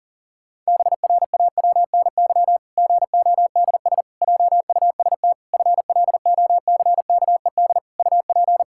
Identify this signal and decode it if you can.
Morse code, heard as a keyed tone, about 40 wpm, 700 Hz. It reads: BRAWNY GODS JUST FLOCKED UP
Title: Pangram Challenge v2 - 40wpm